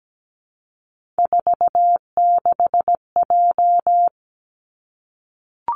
Morse code, heard as a keyed tone, about 17 wpm, 700 Hz. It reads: 46J
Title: Three Challenging Letter-Number Combinations 17wpm